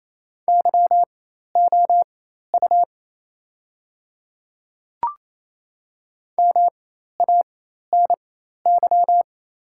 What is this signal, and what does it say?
Morse code, heard as a keyed tone, about 28 wpm, spaced as for 13 wpm, 690 Hz. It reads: YOU MANY
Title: 4x Character Spacing - Top 300 Words - 28wpm